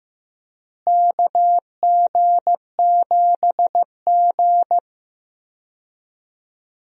Morse code, heard as a keyed tone, about 16 wpm, 700 Hz. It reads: KG7G
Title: USA Call Signs - 4 Characters - Repeated 2x - 15wpm